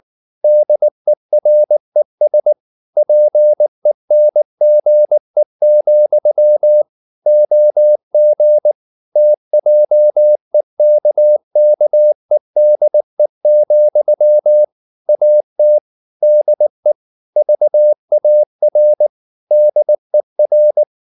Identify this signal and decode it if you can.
Morse code, heard as a keyed tone, about 19 wpm, 600 Hz. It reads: DERES PENGE, OG TJEKKEDE, AT DE VAR DER
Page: Ferie 19 wpm